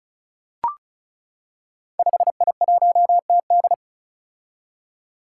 Morse code, heard as a keyed tone, about 36 wpm, 700 Hz. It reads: 5I1TD
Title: Five Challenging Letter-Number Combinations 35wpm